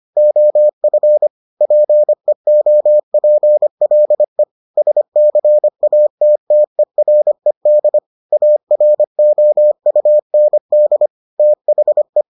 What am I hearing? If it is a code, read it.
Morse code, heard as a keyed tone, about 25 wpm, 600 Hz. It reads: OF PEOPLE SCATTERED AROUND THE